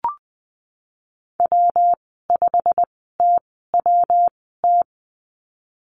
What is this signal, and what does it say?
Morse code, heard as a keyed tone, about 19 wpm, spaced as for 14 wpm, 700 Hz. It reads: W5TWT
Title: 186 Lesson 30g - Letter b - Call Signs 20wpm